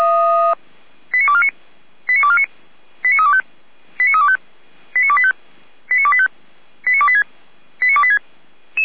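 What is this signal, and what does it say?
Five-tone selective call (ZVEI), heard as quick runs of five tones, tones of about 70 ms. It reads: 89249 89249 89325 89325 89285 89285 89286 89286
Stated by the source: ZVEI five-tone selective call received by SDR.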